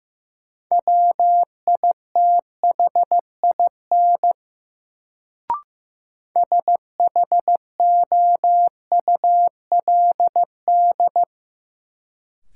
Morse code, heard as a keyed tone, about 15 wpm, 700 Hz. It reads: WITHIN SHOULD
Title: Sets of 2 Words - New Word Drill - 15wpm